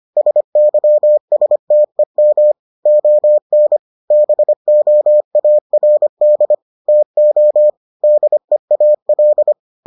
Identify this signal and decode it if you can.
Morse code, heard as a keyed tone, about 25 wpm, 600 Hz. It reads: SYSTEM ON BOARD TO DEAL